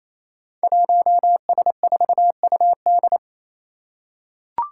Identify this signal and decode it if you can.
Morse code, heard as a keyed tone, about 28 wpm, 700 Hz. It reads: 1S4UD
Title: Five Challenging Letter-Number Combinations 28wpm